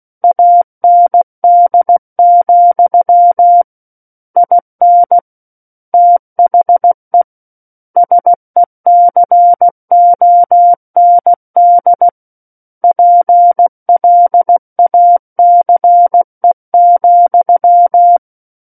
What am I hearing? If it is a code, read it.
Morse code, heard as a keyed tone, about 16 wpm, 700 Hz. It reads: AND, IN THE SECOND PLACE,
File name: Book One / 07-Chapter 7 - 16 WPM